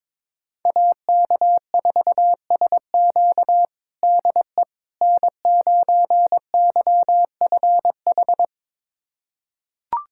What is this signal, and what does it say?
Morse code, heard as a keyed tone, about 21 wpm, 700 Hz. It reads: AK4SQ DE N9YFH
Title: USA Callsign DE Callsign - 5 Characters - 22wpm